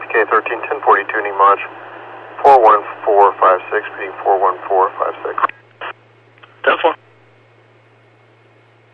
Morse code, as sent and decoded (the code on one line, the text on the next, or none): none